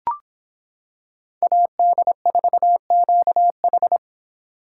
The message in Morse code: AD4QH

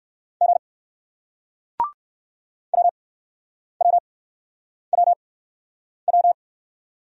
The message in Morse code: G WWWW

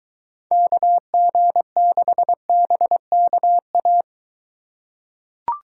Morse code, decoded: KG6BKA

